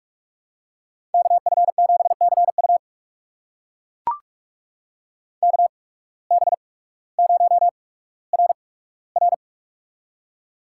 Morse code, heard as a keyed tone, about 44 wpm, spaced as for 19 wpm, 690 Hz. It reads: KF7CU KB0RR
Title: USA Call Signs - 5 Characters - 8x Character Spacing - 45wpm